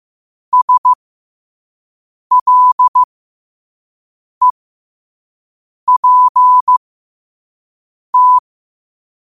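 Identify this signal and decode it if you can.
Morse code, heard as a keyed tone, about 15 wpm, spaced as for 5 wpm, 1000 Hz. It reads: SLEPT
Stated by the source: Quote-of-the-day Morse code practice at 5 wpm.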